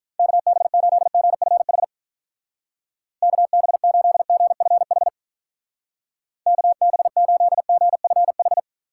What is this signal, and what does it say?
Morse code, heard as a keyed tone, about 42 wpm, 700 Hz. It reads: KB8GFH KB8GFH KB8GFH
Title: Speed Racing - USA 2x3 Call Signs 35wpm